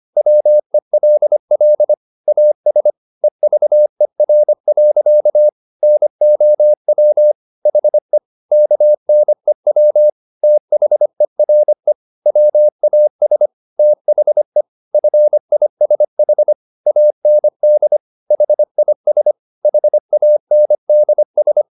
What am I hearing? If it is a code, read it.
Morse code, heard as a keyed tone, about 25 wpm, 600 Hz. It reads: WELL AS EVER. NOW HE KNEW THERE WAS THE FISH AND HIS HANDS